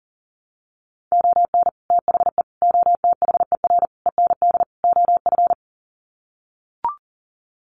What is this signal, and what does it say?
Morse code, heard as a keyed tone, about 40 wpm, 700 Hz. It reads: ON THE OTHER END OF
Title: Sets of 5 Words Encore 40wpm